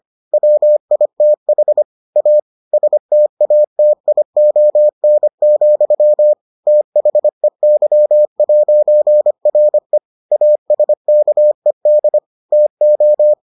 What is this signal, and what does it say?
Morse code, heard as a keyed tone, about 25 wpm, 600 Hz. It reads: WITH A STATION, THEY'RE ASKED TO